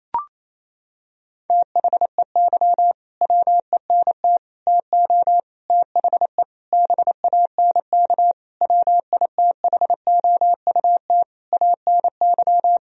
Morse code, heard as a keyed tone, about 28 wpm, 700 Hz. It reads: THEY WENT TO THE BANK WITHOUT ANY
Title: Sentences from Top 1000 Words - Review Part 4 - 28wpm